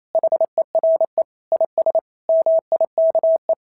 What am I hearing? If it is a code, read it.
Morse code, heard as a keyed tone, about 28 wpm, 690 Hz. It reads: HERE IS MIKE
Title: SOTA QSO Practice 28wpm